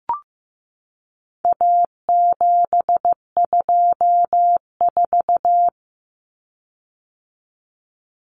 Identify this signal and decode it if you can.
Morse code, heard as a keyed tone, about 15 wpm, 700 Hz. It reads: A724
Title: Common Cut Numbers - 4-Digit 15wpm